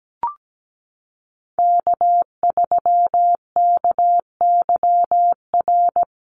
Morse code, heard as a keyed tone, about 17 wpm, 700 Hz. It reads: K3KYR
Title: NA SOTA Chaser Callsigns 17wpm